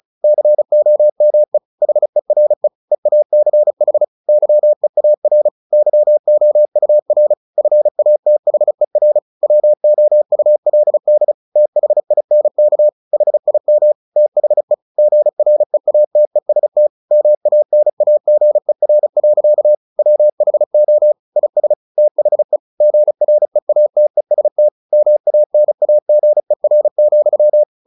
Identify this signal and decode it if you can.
Morse code, heard as a keyed tone, about 35 wpm, 600 Hz. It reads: COME HERE EACH YEAR YOUR FATHER WOULD THINK HIM THE GREATEST MANAGER. WHO IS THE GREATEST MANAGER,
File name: the old man and the sea / the old man and the sea 1 35wpm